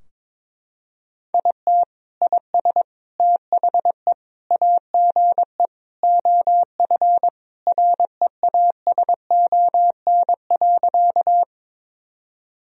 Morse code, heard as a keyed tone, about 22 wpm, 700 Hz. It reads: IT IS THE AGE OF REASON.
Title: Sentences from Top 1000 Words - Review Part 3 - 22wpm